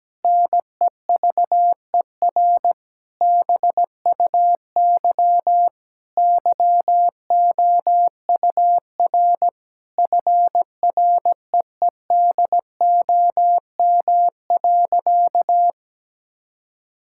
Morse code, heard as a keyed tone, about 17 wpm, 700 Hz. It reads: NEVER BUY YOUR FREEDOM.
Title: Sentences from Top 1000 Words - Review Part 5 - 17wpm